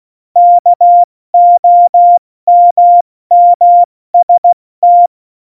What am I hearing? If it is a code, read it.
Morse code, heard as a keyed tone, about 16 wpm, spaced as for 14 wpm, 700 Hz. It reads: KOMMST
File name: Morsekurs / Witze / Witz 27 March 2018, 70 F80